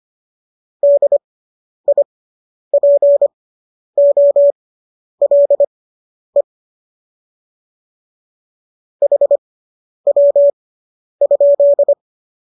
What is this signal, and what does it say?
Morse code, heard as a keyed tone, about 25 wpm, spaced as for 10 wpm, 570 Hz. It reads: DIPOLE HW?